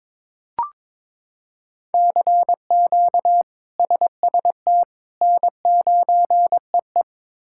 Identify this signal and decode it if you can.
Morse code, heard as a keyed tone, about 21 wpm, 700 Hz. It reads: CQ SST N9EE
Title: K1USN SST QSO Practice - 22wpm